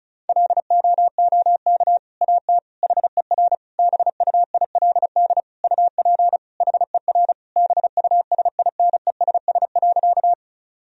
Morse code, heard as a keyed tone, about 36 wpm, 700 Hz. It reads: LOOK AT HER BUILD UP HER BUSINESS.
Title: Sentences from Top 1000 Words - Review Part 1 - 35wpm